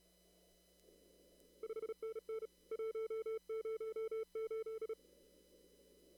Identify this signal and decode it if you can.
Morse code, heard as a keyed tone, about 33 wpm, 410 Hz. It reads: 5NN 108
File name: CW / kt5j wpx cw-op-k5pi